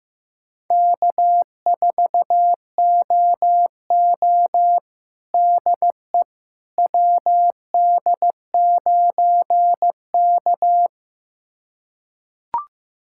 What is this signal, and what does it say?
Morse code, heard as a keyed tone, about 15 wpm, 700 Hz. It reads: K4OO DE WD9K
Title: USA Callsign DE Callsign - 4 Characters - 15wpm